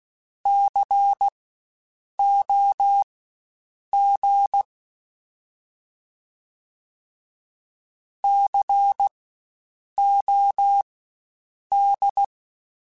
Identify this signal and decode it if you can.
Morse code, heard as a keyed tone, about 16 wpm, spaced as for 7 wpm, 800 Hz. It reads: COG COD
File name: Spaced Alphas / Spaced Alphas mp3s / Cs sp @15